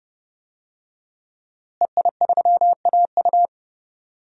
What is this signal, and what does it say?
Morse code, heard as a keyed tone, about 30 wpm, 700 Hz. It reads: EI3AU
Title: DXCC-100 Call Signs - 5 Characters - Repeated 2x - 30wpm